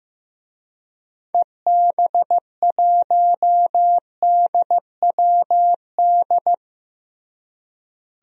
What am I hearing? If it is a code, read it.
Morse code, heard as a keyed tone, about 15 wpm, 700 Hz. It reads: EB1DWD
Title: DXCC-100 Call Signs - 6 Characters - Repeated 3x - 15wpm